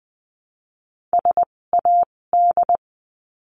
Morse code, heard as a keyed tone, about 20 wpm, spaced as for 16 wpm, 700 Hz. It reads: SAD
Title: 256 Lesson 42a - Words 12-20wpm